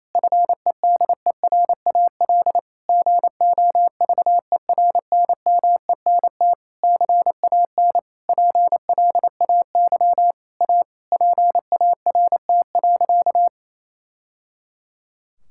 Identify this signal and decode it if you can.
Morse code, heard as a keyed tone, about 28 wpm, 700 Hz. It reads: FEDERAL GOVERNMENT CAN PLAY A PART.
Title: Sentences from Top 1000 Words - Review Part 5 - 28wpm